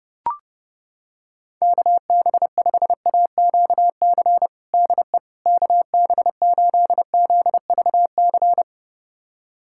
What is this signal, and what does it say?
Morse code, heard as a keyed tone, about 30 wpm, 700 Hz. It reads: KB5AQC DE KB8ZVC
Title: USA Callsign DE Callsign - 6 Characters - 30wpm